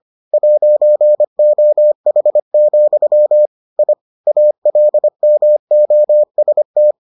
Morse code, heard as a keyed tone, about 25 wpm, 600 Hz. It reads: 'OH, I ALMOST